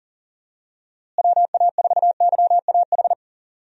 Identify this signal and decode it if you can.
Morse code, heard as a keyed tone, about 37 wpm, 700 Hz. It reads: WA4YAH
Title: USA Call Signs - 6 Characters - Repeated 2x - 40wpm